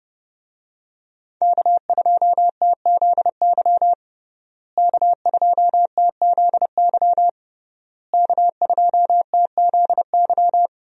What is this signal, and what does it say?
Morse code, heard as a keyed tone, about 30 wpm, 700 Hz. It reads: K2TZY K2TZY K2TZY
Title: USA Call Signs - 5 Characters - Repeated 3x - 30wpm